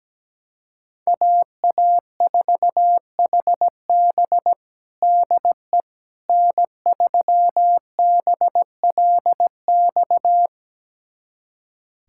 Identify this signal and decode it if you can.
Morse code, heard as a keyed tone, about 17 wpm, 700 Hz. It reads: AA4HB DE N3BLX
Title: USA Callsign DE Callsign - 5 Characters - 17wpm